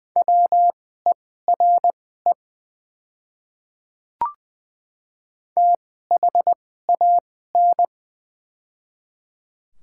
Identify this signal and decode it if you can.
Morse code, heard as a keyed tone, about 20 wpm, spaced as for 15 wpm, 690 Hz. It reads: WERE THAN
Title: 2x Character Spacing - Top 300 Words - 20wpm